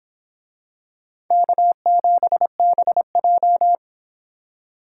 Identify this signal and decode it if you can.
Morse code, heard as a keyed tone, about 24 wpm, 700 Hz. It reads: K7BJ